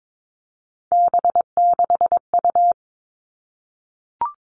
Morse code, heard as a keyed tone, about 22 wpm, 700 Hz. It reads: B6U
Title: Three Challenging Letter-Number Combinations 22wpm